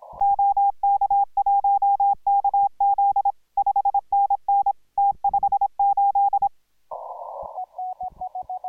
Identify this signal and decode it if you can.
Morse code, heard as a keyed tone, about 25 wpm, 810 Hz. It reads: OK1KZ 5NN T58